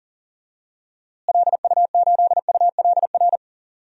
Morse code, heard as a keyed tone, about 37 wpm, 700 Hz. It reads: LU8ULR
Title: DXCC-100 Call Signs - 6 Characters - Repeated 3x - 40wpm